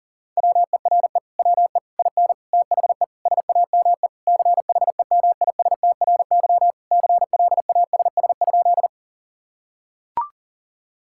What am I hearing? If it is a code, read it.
Morse code, heard as a keyed tone, about 40 wpm, 700 Hz. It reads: WERE WE IN THE SAME CHEMISTRY CLASS?
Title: Sentences from Top 1000 Words - Review Part 6 - 40wpm